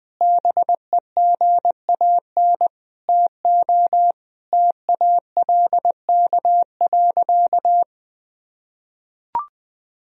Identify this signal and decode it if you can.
Morse code, heard as a keyed tone, about 20 wpm, 700 Hz. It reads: BEGAN TO TALK.
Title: Sentences from Top 1000 Words - Review Part 1 - 20wpm